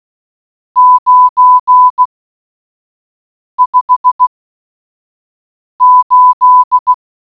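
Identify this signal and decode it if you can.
Morse code, heard as a keyed tone, about 16 wpm, 1000 Hz. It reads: 9 5 8
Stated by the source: Morse audio clue in a Codebreaker Portal puzzle game, AIT Forensics Dashboard.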